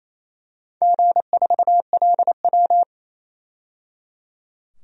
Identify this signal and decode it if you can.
Morse code, heard as a keyed tone, about 28 wpm, 700 Hz. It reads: G4LW
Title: DXCC-100 Call Signs - 4 Characters - 28wpm